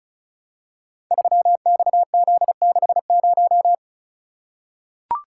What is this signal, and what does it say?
Morse code, heard as a keyed tone, about 36 wpm, 700 Hz. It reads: 3XZ60
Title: Code Groups - 5 Letter:Numbers - 35wpm